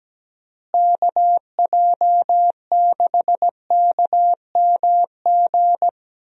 Morse code, heard as a keyed tone, about 17 wpm, 700 Hz. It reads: KJ6KMG
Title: USA Call Signs - 6 Characters - Repeated 2x - 17wpm